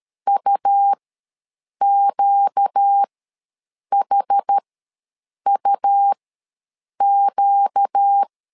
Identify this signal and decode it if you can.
Morse code, heard as a keyed tone, about 13 wpm, spaced as for 7 wpm, 800 Hz. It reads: UQHUQ
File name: LEZIONI / 4 Q ws 12wpm 1912